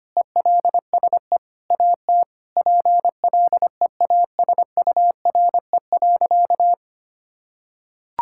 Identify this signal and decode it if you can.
Morse code, heard as a keyed tone, about 25 wpm, 700 Hz. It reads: ELSE AT PLEASURE.
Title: Sentences from Top 1600 Words 25wpm